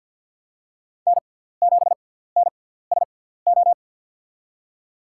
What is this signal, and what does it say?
Morse code, heard as a keyed tone, about 53 wpm, spaced as for 17 wpm, 700 Hz. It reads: N7NSO
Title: USA Call Signs - 5 Characters - 6x Character Spacing - 50wpm